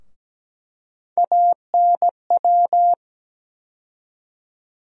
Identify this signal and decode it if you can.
Morse code, heard as a keyed tone, about 17 wpm, 700 Hz. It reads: ANW
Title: USA Call Sign Postfixes - 3 Letters 17wpm